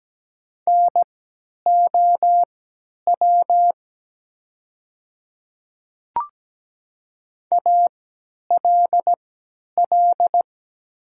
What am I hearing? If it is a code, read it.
Morse code, heard as a keyed tone, about 17 wpm, spaced as for 10 wpm, 700 Hz. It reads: NOW ALL
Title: 3x Character Spacing - Top 500 Words - 17wpm